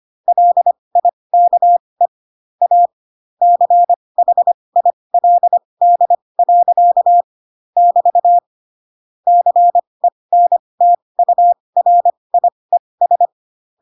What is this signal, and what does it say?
Morse code, heard as a keyed tone, about 25 wpm, spaced as for 20 wpm, 700 Hz. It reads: LIKE A CHILD. <BT> CENTURIES
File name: Pinocchio / Pinocchio ch1 15 25 ik1hge